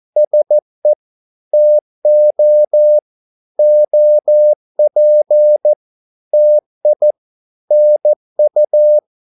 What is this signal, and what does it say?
Morse code, heard as a keyed tone, about 14 wpm, 600 Hz. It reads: SE TO OP TI NU
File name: Korte ord DK 14wpm